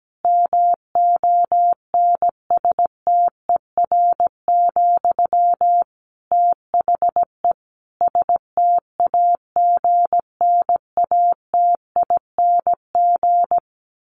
Morse code, heard as a keyed tone, about 17 wpm, 700 Hz. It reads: MONSTER, THE STAGNATING
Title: A Sound Of Thunder 17wpm